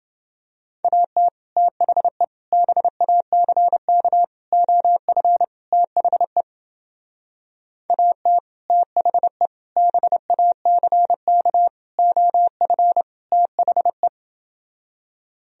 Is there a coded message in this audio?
Yes: Morse code from 0.8 to 14.1 s, heard as a keyed tone, about 30 wpm, 700 Hz. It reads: AT THE BACK OF THE AT THE BACK OF THE